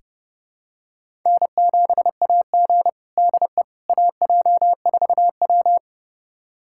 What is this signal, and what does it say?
Morse code, heard as a keyed tone, about 30 wpm, 700 Hz. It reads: N7AG DE AJ4W
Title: USA Callsign DE Callsign - 4 Characters - 30wpm